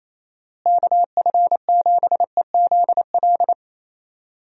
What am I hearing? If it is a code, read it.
Morse code, heard as a keyed tone, about 28 wpm, 700 Hz. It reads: KF7EZL
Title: USA Call Signs - 6 Characters - Repeated 2x - 28wpm